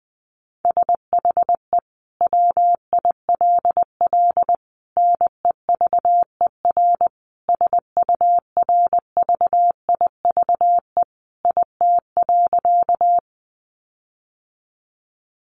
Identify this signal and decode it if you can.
Morse code, heard as a keyed tone, about 20 wpm, 700 Hz. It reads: SHE WILL NEVER SURVIVE IT.
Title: Sentences from Top 1300 Words 20wpm